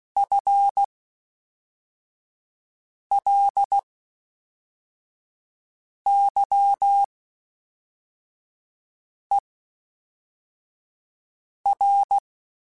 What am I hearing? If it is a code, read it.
Morse code, heard as a keyed tone, about 16 wpm, spaced as for 4 wpm, 800 Hz. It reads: FLYER
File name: Spaced / FL Runs sp mp3s / FL Runs sp @5